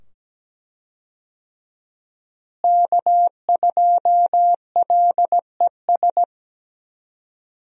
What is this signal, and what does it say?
Morse code, heard as a keyed tone, about 17 wpm, 700 Hz. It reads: K2LES
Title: USA Call Signs - 5 Characters - Repeated 2x - 17wpm